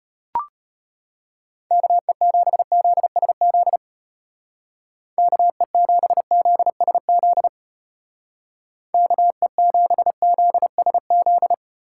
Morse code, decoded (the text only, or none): KE7ZSZ KE7ZSZ KE7ZSZ